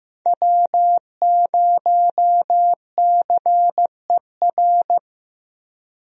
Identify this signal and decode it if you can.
Morse code, heard as a keyed tone, about 15 wpm, 700 Hz. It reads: W0CER
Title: USA Call Signs - 5 Characters - Repeated 3x - 15wpm